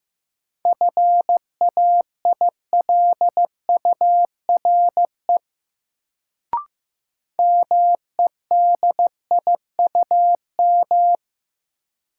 Morse code, heard as a keyed tone, about 15 wpm, 700 Hz. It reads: FAILURE MEDIUM
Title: Sentences from Top 900 Words - New Word Drill - 15wpm